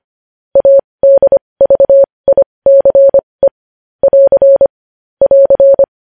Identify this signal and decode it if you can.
Morse code, heard as a keyed tone, about 25 wpm, spaced as for 20 wpm, 550 Hz. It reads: ADVICE <AR> <AR>
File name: podcast / News headlines-1x-25x20